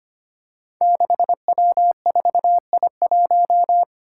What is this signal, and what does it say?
Morse code, heard as a keyed tone, about 25 wpm, 700 Hz. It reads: 6W4I1
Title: Five Challenging Letter-Number Combinations 25wpm